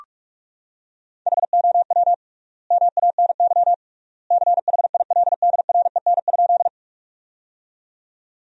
Morse code, heard as a keyed tone, about 41 wpm, 700 Hz. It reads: HOW MANY CHILDREN?
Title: 3x Word-Spacing - Sentences from Top 300 Words - 45wpm